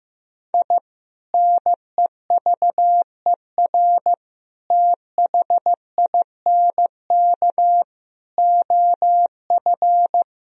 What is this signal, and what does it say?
Morse code, heard as a keyed tone, about 15 wpm, 700 Hz. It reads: I NEVER THINK OF
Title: Sentences from Top 1000 Words - Review Part 1 - 15wpm